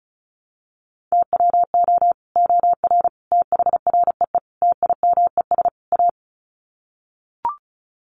Morse code, heard as a keyed tone, about 34 wpm, 700 Hz. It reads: TWO OR THREE TIMES A
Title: Sets of 5 Words Encore2 35wpm